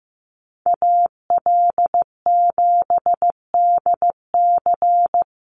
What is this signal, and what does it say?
Morse code, heard as a keyed tone, about 15 wpm, 700 Hz. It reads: AL7DC
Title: USA Call Signs 15wpm